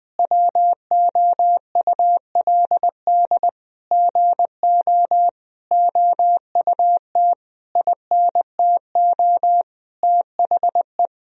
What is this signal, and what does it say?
Morse code, heard as a keyed tone, about 20 wpm, 700 Hz. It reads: WOULD GO OUT INTO THE
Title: Sentences from Top 1000 Words - Review Part 3 - 20wpm